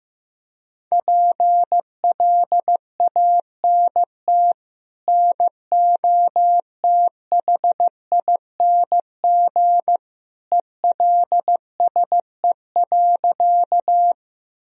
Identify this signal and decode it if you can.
Morse code, heard as a keyed tone, about 15 wpm, 700 Hz. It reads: PLANT NOTHING ELSE.
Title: Sentences from Top 1000 Words - Review Part 1 - 15wpm